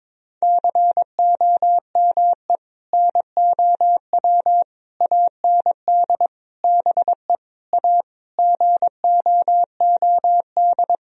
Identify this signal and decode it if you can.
Morse code, heard as a keyed tone, about 22 wpm, 700 Hz. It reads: COME NOW AND BE A GOOD
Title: Sentences from Top 1000 Words - Review Part 4 - 22wpm